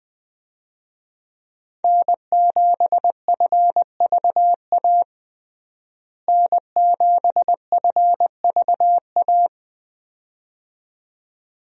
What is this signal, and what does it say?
Morse code, heard as a keyed tone, about 20 wpm, 700 Hz. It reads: N7FVA N7FVA
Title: USA Call Signs - 5 Characters - Repeated 2x - 20wpm